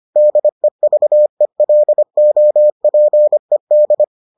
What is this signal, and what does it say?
Morse code, heard as a keyed tone, about 25 wpm, 600 Hz. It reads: DEVELOPED